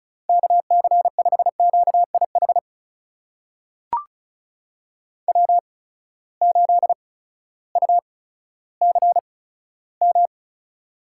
Morse code, heard as a keyed tone, about 35 wpm, spaced as for 16 wpm, 700 Hz. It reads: KC5QIH W8UCM
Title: USA Call Signs - 8x Character Spacing - 35wpm